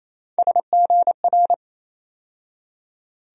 Morse code, heard as a keyed tone, about 27 wpm, 700 Hz. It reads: SGR
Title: USA Call Sign Postfixes - 3 Letters 28wpm